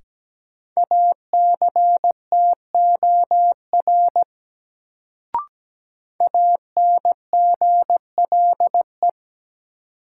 Morse code, heard as a keyed tone, about 17 wpm, 700 Hz. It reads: ACTOR ANGLE
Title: Sentences from Top 1500 Words - New Word Drill - 17wpm